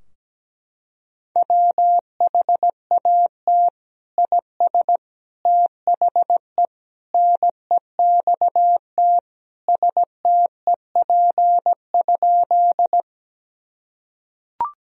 Morse code, heard as a keyed tone, about 17 wpm, 700 Hz. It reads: WHAT IS THE NEXT STEP?
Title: Sentences from Top 1000 Words - Review Part 2 - 17wpm